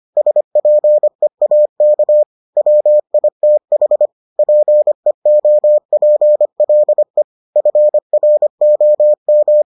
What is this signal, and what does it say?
Morse code, heard as a keyed tone, about 25 wpm, 600 Hz. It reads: SPEAK WITH PEOPLE FROM